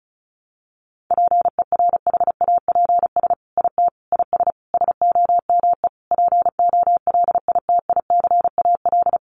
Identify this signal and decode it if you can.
Morse code, heard as a keyed tone, about 34 wpm, 700 Hz. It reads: PERHAPS IT IS SOME POLITICAL
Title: Sentences from Top 800 Words 35wpm